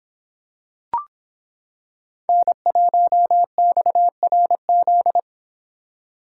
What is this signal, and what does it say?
Morse code, heard as a keyed tone, about 26 wpm, 700 Hz. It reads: N1XRZ